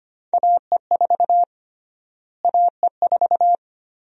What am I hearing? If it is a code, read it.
Morse code, heard as a keyed tone, about 25 wpm, 700 Hz. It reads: AE4 AE4
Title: USA Call Sign Prefixes - Repeated 2x - 25wpm